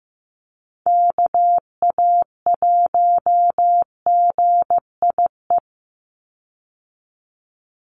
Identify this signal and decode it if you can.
Morse code, heard as a keyed tone, about 15 wpm, 700 Hz. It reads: KA1GIE
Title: USA Call Signs 15wpm